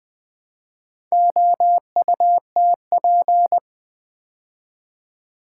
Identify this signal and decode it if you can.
Morse code, heard as a keyed tone, about 20 wpm, 700 Hz. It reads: OUTP
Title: Build A Word - 2 Letters At A Time - 6-Letter Words - 20wpm